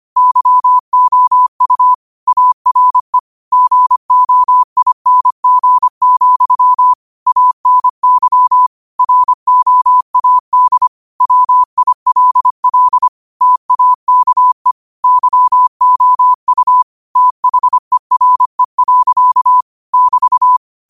Morse code, heard as a keyed tone, about 25 wpm, 1000 Hz. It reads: YOU ARE GOING, ANY ROAD WILL TAKE YOU THERE. <BT>
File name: qotdfiles / 25wpm